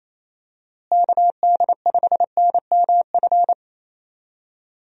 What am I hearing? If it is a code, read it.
Morse code, heard as a keyed tone, about 28 wpm, 700 Hz. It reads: KD5NMF
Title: USA Call Signs - 6 Characters - Repeated 3x - 28wpm